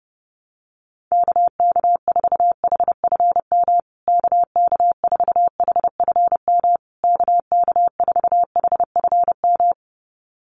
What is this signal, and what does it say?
Morse code, heard as a keyed tone, about 30 wpm, 700 Hz. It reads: KK4HFM KK4HFM KK4HFM
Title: USA Call Signs Repeated 3x 30wpm